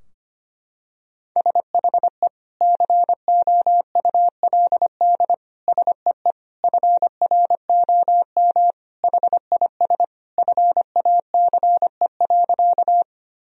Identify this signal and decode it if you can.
Morse code, heard as a keyed tone, about 25 wpm, 700 Hz. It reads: SHE COULD SEE FROM HIS FACE.
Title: Sentences from Top 1000 Words - Review Part 2 - 25wpm